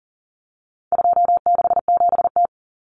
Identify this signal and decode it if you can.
Morse code, heard as a keyed tone, about 37 wpm, 700 Hz. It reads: 267T
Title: Common Cut Numbers - 4-Digit 40wpm